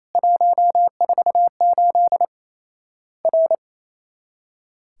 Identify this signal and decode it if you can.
Morse code, heard as a keyed tone, about 28 wpm, 700 Hz. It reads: 148 R
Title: Worked All Europe DX Contest - QTC Exchange - 28wpm